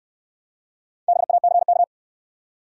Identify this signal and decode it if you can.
Morse code, heard as a keyed tone, about 63 wpm, 700 Hz. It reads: BACK